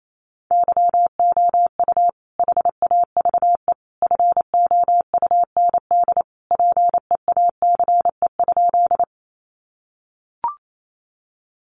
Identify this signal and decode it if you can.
Morse code, heard as a keyed tone, about 28 wpm, 700 Hz. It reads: YOU HAVE FOUND PEACE?
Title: Sentences from Top 1500 Words 28wpm